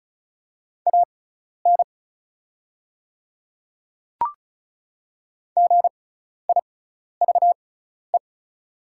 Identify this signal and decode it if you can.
Morse code, heard as a keyed tone, about 36 wpm, spaced as for 12 wpm, 700 Hz. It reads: AN GIVE
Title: 6x Character Spacing - Top 200 Words - 35wpm